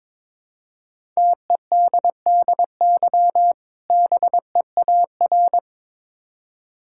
Morse code, heard as a keyed tone, about 22 wpm, 690 Hz. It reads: TEDDY BEAR
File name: Difficult Words with Hints - 22wpm